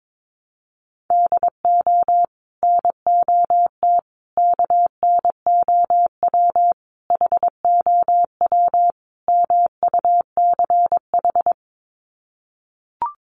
Morse code, heard as a keyed tone, about 22 wpm, 700 Hz. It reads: DO NOT KNOW HOW MUCH
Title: Sets of 5 Words Encore 22wpm